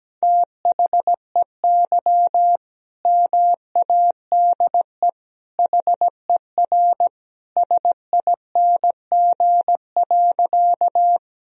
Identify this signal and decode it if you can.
Morse code, heard as a keyed tone, about 17 wpm, 700 Hz. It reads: THEY MADE HER SING.
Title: Sentences from Top 1000 Words - Review Part 2 - 17wpm